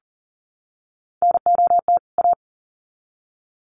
Morse code, heard as a keyed tone, about 40 wpm, 700 Hz. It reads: NOT A
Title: Sets of 2 Words 40wpm